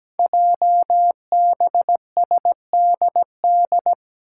Morse code, heard as a keyed tone, about 17 wpm, 700 Hz. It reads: JBSDD
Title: Five Challenging Letter-Number Combinations 17wpm